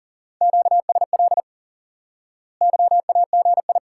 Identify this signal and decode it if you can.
Morse code, heard as a keyed tone, about 40 wpm, 700 Hz. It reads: QSL YAGI
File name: course / instant-qso-element-course-3 / IQE Course 3 - Lesson 89 - YAGI - Two Elements - 40wpm